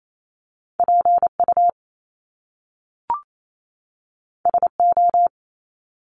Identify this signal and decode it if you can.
Morse code, heard as a keyed tone, about 28 wpm, 700 Hz. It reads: PU SO
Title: Top Two Letter Combinations in Words 28wpm